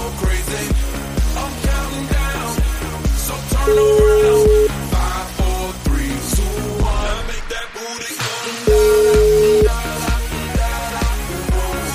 Ringback tone (Europe, 425 Hz), with noise.